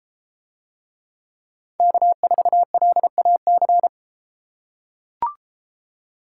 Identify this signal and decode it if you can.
Morse code, heard as a keyed tone, about 32 wpm, 700 Hz. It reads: K4LAC